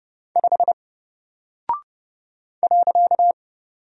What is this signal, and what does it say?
Morse code, heard as a keyed tone, about 30 wpm, 700 Hz. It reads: 5 .
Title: Single Character (Letter:Number:Punctuation:Prosign) 30wpm